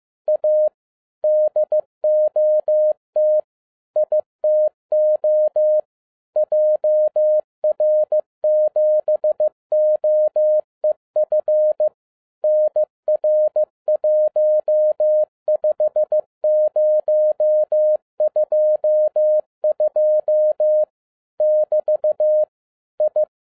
Morse code, heard as a keyed tone, about 15 wpm, 610 Hz. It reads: A DOT ITO JR7OEF NR15022 <BT> I
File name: 2011-10-mp3 / n07-A DOT 15wpm.mp3